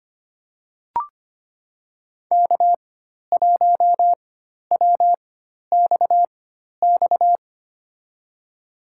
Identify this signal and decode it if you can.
Morse code, heard as a keyed tone, about 25 wpm, spaced as for 12 wpm, 700 Hz. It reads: K1WXX
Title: USA Call Signs - 5 Characters - 4x Character Spacing - 25wpm